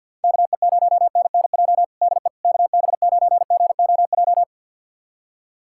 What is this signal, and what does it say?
Morse code, heard as a keyed tone, about 51 wpm, 700 Hz. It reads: KE0NNJ DE KB9GOJ
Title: USA Callsign DE Callsign - 6 Characters - 50wpm